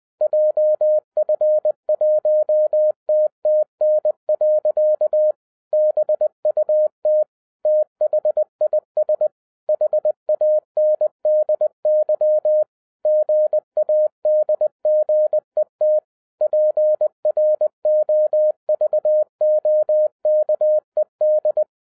Morse code, heard as a keyed tone, about 20 wpm, 610 Hz. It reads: JF1TTN. BUT THIS HANDY GADGET PROVOKED
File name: 2012-01-mp3 / n05-CAME BACK TO HAM RADIO AFTER 30 YEARS OF INTERVAL-20wpm